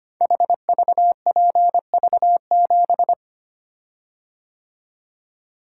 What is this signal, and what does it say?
Morse code, heard as a keyed tone, about 25 wpm, 700 Hz. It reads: HVPV7
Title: Code Groups - 5 Letter:Numbers - 25wpm